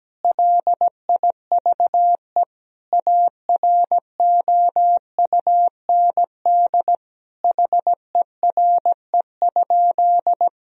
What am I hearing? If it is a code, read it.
Morse code, heard as a keyed tone, about 17 wpm, 700 Hz. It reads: LIVE AROUND HERE?
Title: Sentences from Top 1000 Words - Review Part 3 - 17wpm